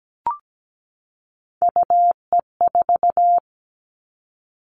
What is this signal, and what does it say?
Morse code, heard as a keyed tone, about 17 wpm, 700 Hz. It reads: UE4
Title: Three Challenging Letter-Number Combinations 17wpm